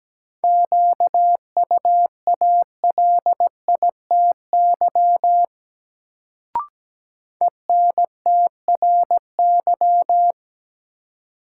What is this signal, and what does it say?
Morse code, heard as a keyed tone, about 17 wpm, 700 Hz. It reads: QUALITY ENTRY
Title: Sentences from Top 1500 Words - New Word Drill Alt - 17wpm